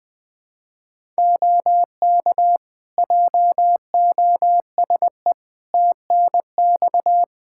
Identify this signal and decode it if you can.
Morse code, heard as a keyed tone, about 20 wpm, 700 Hz. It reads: OK JOSE TNX
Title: K1USN SST QSO Practice 20wpm